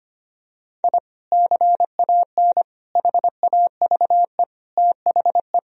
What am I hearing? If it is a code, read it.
Morse code, heard as a keyed tone, about 25 wpm, 700 Hz. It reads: I CAN HAVE THE
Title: Sentences from Top 1000 Words - Review Part 2 - 25wpm